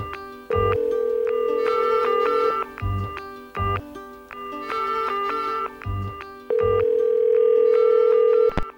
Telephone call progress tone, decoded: ringback tone